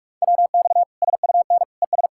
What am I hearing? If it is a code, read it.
Morse code, heard as a keyed tone, about 41 wpm, 700 Hz. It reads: WX SUN ES
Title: Practice QSO - Exchange 2 - Info - 45wpm